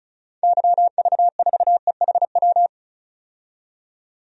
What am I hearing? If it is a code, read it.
Morse code, heard as a keyed tone, about 36 wpm, 700 Hz. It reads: YV4EHW
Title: DXCC-100 Call Signs - 6 Characters - 35wpm